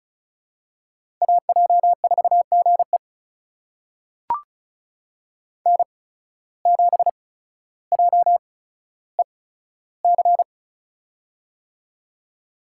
Morse code, heard as a keyed tone, about 36 wpm, spaced as for 15 wpm, 700 Hz. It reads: AJ4GE N7JEC
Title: USA Call Signs - 5 Characters - 8x Character Spacing - 35wpm